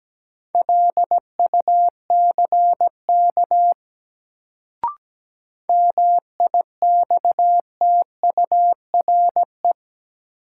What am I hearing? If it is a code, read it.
Morse code, heard as a keyed tone, about 17 wpm, 700 Hz. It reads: LUCK MIXTURE
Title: Sentences from Top 1500 Words - New Word Drill Alt - 17wpm